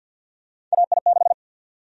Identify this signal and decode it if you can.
Morse code, heard as a keyed tone, about 49 wpm, 700 Hz. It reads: AI6
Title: USA Call Sign Prefixes - 50wpm